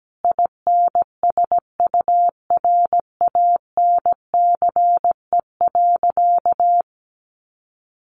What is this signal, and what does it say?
Morse code, heard as a keyed tone, about 17 wpm, 700 Hz. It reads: INSURANCE.
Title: Sentences from Top 1100 Words 17wpm